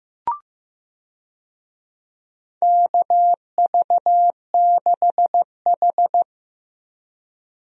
Morse code, heard as a keyed tone, about 15 wpm, 700 Hz. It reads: KV6H
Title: USA Call Signs - 4 Characters - Repeated 3x - 15wpm